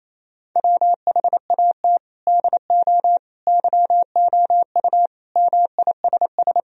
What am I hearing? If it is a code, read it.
Morse code, heard as a keyed tone, about 28 wpm, 700 Hz. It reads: WHAT DO YOU MISS
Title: Sentences from Top 1000 Words - Review Part 3 - 28wpm